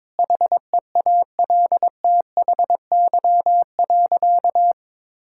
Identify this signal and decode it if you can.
Morse code, heard as a keyed tone, about 22 wpm, 700 Hz. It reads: HEALTHY.
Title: Sentences from Top 1000 Words - Review Part 5 - 22wpm